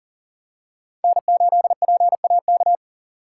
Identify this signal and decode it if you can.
Morse code, heard as a keyed tone, about 40 wpm, 700 Hz. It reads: N8PAK
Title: USA Call Signs - 5 Characters - 2x Character Spacing - 40wpm